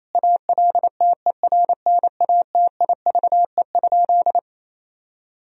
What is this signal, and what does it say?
Morse code, heard as a keyed tone, about 28 wpm, 700 Hz. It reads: ALTERNATIVE?
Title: Sentences from Top 1000 Words - Review Part 4 - 28wpm